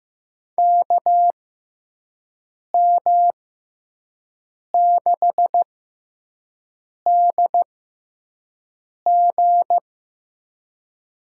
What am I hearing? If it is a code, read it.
Morse code, heard as a keyed tone, about 15 wpm, spaced as for 5 wpm, 700 Hz. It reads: KM6DG
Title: USA Call Signs - 6x Character Spacing - 15wpm